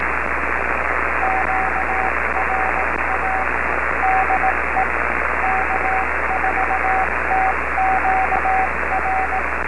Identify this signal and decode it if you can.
Morse code, heard as a keyed tone, about 18 wpm, 760 Hz, with radio noise. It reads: QRA DE K4TQR